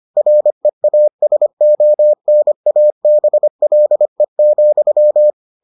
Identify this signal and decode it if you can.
Morse code, heard as a keyed tone, about 25 wpm, 600 Hz. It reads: REASONABLE,